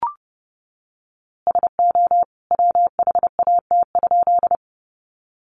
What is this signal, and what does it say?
Morse code, heard as a keyed tone, about 29 wpm, 700 Hz. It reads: SO WHAT?
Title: A Sound Of Thunder 30wpm